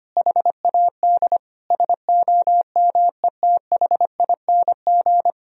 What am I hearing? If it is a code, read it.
Morse code, heard as a keyed tone, about 25 wpm, 700 Hz. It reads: HAD SOMETHING
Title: Sentences from Top 1000 Words - Review Part 1 - 25wpm